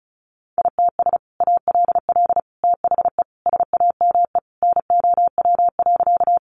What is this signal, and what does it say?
Morse code, heard as a keyed tone, about 34 wpm, 700 Hz. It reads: ITS ALL THE SAME NOW.
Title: Sentences from Top 600 Words 35wpm